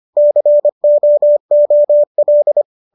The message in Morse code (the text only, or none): COOL